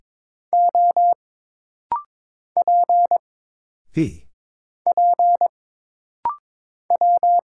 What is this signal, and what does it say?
Morse code, heard as a keyed tone, about 22 wpm, 700 Hz. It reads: O P P W
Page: This rapid-fire timing is geared to push your ICR (Instant Character Recognition) skills.